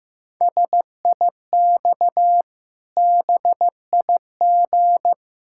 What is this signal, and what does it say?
Morse code, heard as a keyed tone, about 15 wpm, 700 Hz. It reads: SIX BIG
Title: Pangram Challenge v2 - 15wpm